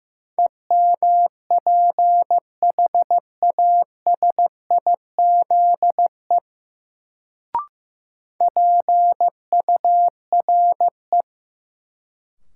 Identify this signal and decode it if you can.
Morse code, heard as a keyed tone, about 15 wpm, 700 Hz. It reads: EMPHASIZE PURE
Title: Sentences from Top 1500 Words - New Word Drill Alt - 15wpm